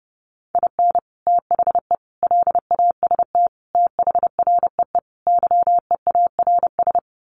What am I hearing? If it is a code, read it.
Morse code, heard as a keyed tone, about 30 wpm, 700 Hz. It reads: IN THE LAST THREE YEARS
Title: Sets of 5 Words Encore2 30wpm